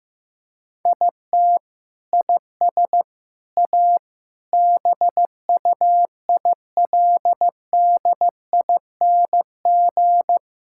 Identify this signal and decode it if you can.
Morse code, heard as a keyed tone, about 15 wpm, 700 Hz. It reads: IT IS A BUILDING
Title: Sentences from Top 1000 Words - Review Part 5 - 15wpm